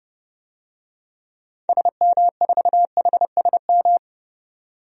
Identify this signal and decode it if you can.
Morse code, heard as a keyed tone, about 32 wpm, 700 Hz. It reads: SM4HSM
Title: DXCC-100 Call Signs - 6 Characters - 30wpm